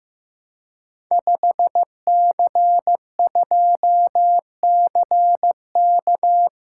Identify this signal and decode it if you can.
Morse code, heard as a keyed tone, about 15 wpm, 700 Hz. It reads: 5C2CK
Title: Code Groups - 5 Letter:Numbers - 15wpm